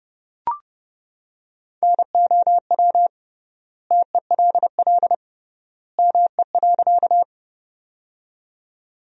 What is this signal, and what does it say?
Morse code, heard as a keyed tone, about 30 wpm, 700 Hz. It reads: NOW TELL ME.
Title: 3x Word-Spacing - Sentences from Top 200 Words - 30wpm